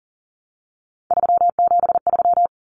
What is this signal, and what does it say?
Morse code, heard as a keyed tone, about 40 wpm, 700 Hz. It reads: 373
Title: Numbers - 3-Digit 40wpm